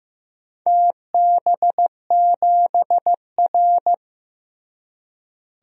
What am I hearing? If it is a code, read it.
Morse code, heard as a keyed tone, about 15 wpm, 700 Hz. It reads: TB7R
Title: DXCC-100 Call Signs - 4 Characters - Repeated 3x - 15wpm